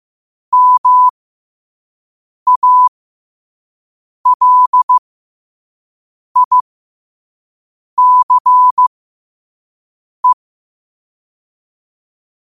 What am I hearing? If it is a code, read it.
Morse code, heard as a keyed tone, about 15 wpm, spaced as for 5 wpm, 1000 Hz. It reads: MALICE
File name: qotdfiles / 5wpm